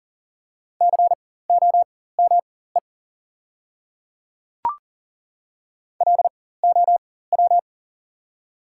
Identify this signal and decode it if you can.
Morse code, heard as a keyed tone, about 40 wpm, spaced as for 19 wpm, 700 Hz. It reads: COME LOW
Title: 4x Character Spacing - Top 200 Words - 40wpm